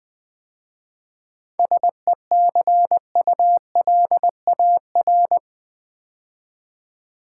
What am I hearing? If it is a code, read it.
Morse code, heard as a keyed tone, about 20 wpm, 700 Hz. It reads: SECULAR
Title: 7 Letter Words - First 3 Letters Given - 20wpm